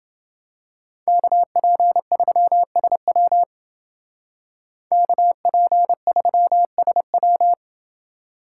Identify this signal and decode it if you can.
Morse code, heard as a keyed tone, about 29 wpm, 700 Hz. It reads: KP3SW KP3SW